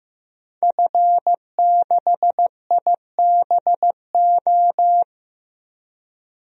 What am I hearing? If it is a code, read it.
Morse code, heard as a keyed tone, about 15 wpm, 700 Hz. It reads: F6IBO
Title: DXCC-100 Call Signs - 5 Characters - Repeated 2x - 15wpm